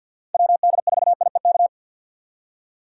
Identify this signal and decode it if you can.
Morse code, heard as a keyed tone, about 48 wpm, 700 Hz. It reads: WD4IEK
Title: USA Call Signs - 6 Characters - 4x Character Spacing - 50wpm